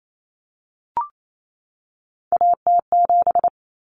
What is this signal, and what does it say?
Morse code, heard as a keyed tone, about 28 wpm, 700 Hz. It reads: AT7
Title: Common Cut Numbers - 3-Digit 28wpm